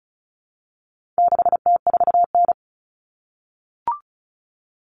Morse code, heard as a keyed tone, about 34 wpm, 700 Hz. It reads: 6T4N
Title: Common Cut Numbers - 4-Digit 35wpm